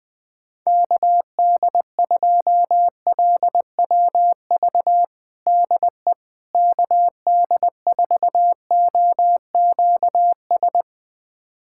Morse code, heard as a keyed tone, about 20 wpm, 700 Hz. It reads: KD2LWV DE KD4OQS
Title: USA Callsign DE Callsign - 6 Characters - 20wpm